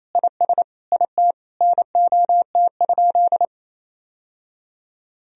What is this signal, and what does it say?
Morse code, heard as a keyed tone, about 28 wpm, 700 Hz. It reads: IS IT NOT?
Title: Sentences from Top 1000 Words - Review Part 1 - 28wpm